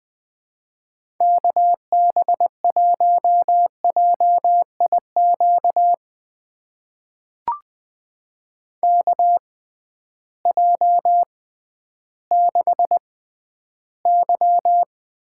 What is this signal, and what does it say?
Morse code, heard as a keyed tone, about 20 wpm, spaced as for 12 wpm, 700 Hz. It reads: KB1JIQ KJ6Y
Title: USA Call Signs - 6 Characters - 6x Character Spacing - 20wpm